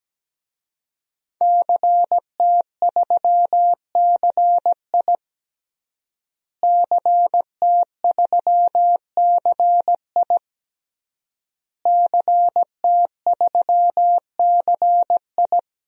Morse code, heard as a keyed tone, about 17 wpm, 700 Hz. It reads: CT3CI CT3CI CT3CI
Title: DXCC-100 Call Signs - 5 Characters - Repeated 3x - 17wpm